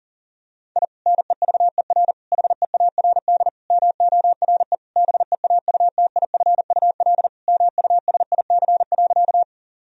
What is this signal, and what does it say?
Morse code, heard as a keyed tone, about 37 wpm, 700 Hz. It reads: I NEVER HEARD MORE BEAUTIFUL MUSIC.
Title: Sentences from Top 1000 Words - Review Part 1 - 40wpm